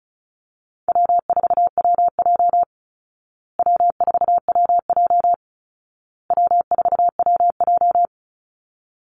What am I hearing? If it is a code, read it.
Morse code, heard as a keyed tone, about 34 wpm, 700 Hz. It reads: W4WJ W4WJ W4WJ
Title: Easy Call Signs Repeated 3x - 35wpm